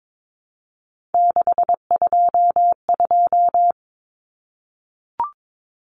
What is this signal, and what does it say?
Morse code, heard as a keyed tone, about 22 wpm, 700 Hz. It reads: 622
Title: Common Cut Numbers - 3-Digit 22wpm